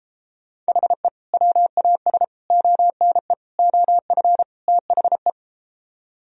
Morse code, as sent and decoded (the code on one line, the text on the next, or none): .... / .-- .- ... / --- -. / --- ..-. / - .... .
HE WAS ONE OF THE